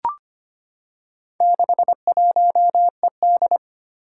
Morse code, heard as a keyed tone, about 25 wpm, 700 Hz. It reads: 61ED
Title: Four Challenging Letter-Number Combinations 25wpm